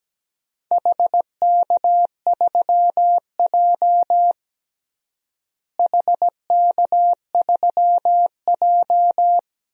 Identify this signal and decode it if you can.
Morse code, heard as a keyed tone, about 17 wpm, 700 Hz. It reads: HK3J HK3J
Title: DXCC-100 Call Signs - 4 Characters - Repeated 2x - 17wpm